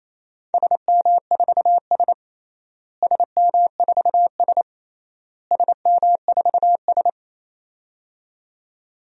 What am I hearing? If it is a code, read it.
Morse code, heard as a keyed tone, about 27 wpm, 700 Hz. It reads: SM4S SM4S SM4S
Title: DXCC-100 Call Signs - 4 Characters - Repeated 3x - 28wpm